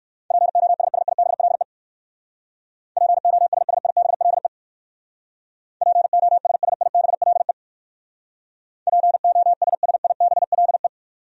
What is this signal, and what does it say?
Morse code, heard as a keyed tone, about 65 wpm, 700 Hz. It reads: POSSIBLE POSSIBLE POSSIBLE POSSIBLE